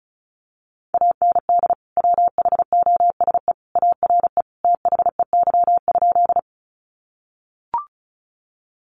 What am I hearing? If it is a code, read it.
Morse code, heard as a keyed tone, about 34 wpm, 700 Hz. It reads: AND WHOSE ARE THEY?
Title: Sentences from Top 1200 Words 35wpm